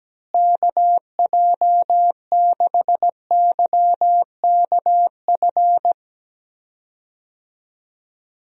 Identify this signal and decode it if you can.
Morse code, heard as a keyed tone, about 17 wpm, 700 Hz. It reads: KJ6YKF
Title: USA Call Signs - 6 Characters - Repeated 2x - 17wpm